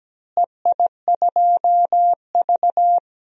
Morse code, heard as a keyed tone, about 17 wpm, 700 Hz. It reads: EI2V
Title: DXCC-100 Call Signs - 4 Characters - Repeated 3x - 17wpm